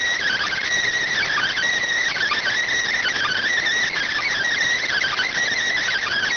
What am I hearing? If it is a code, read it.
Emergency Alert System (SAME) tones (header bursts), heard as harsh buzzing bursts. It is heard from 1.1 to 5.4 s.
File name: findings / S ss-portrait.gen-19 rx